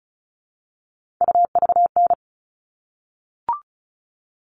Morse code, heard as a keyed tone, about 34 wpm, 700 Hz. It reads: UVN
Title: All Cut Numbers - 3-Digit 35wpm